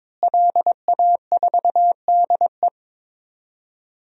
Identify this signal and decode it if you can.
Morse code, heard as a keyed tone, about 22 wpm, 700 Hz. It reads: LA4DE
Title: DXCC-100 Call Signs - 5 Characters - Repeated 2x - 22wpm